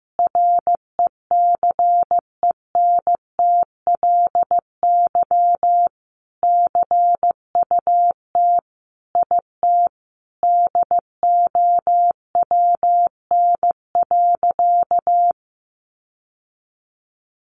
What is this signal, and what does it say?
Morse code, heard as a keyed tone, about 15 wpm, 700 Hz. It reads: RECENTLY CUT IT DOWN.
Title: Sentences from Top 1200 Words 15wpm